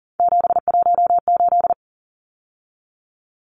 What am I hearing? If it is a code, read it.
Morse code, heard as a keyed tone, about 38 wpm, 700 Hz. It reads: 718
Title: Numbers - 3-Digit 40wpm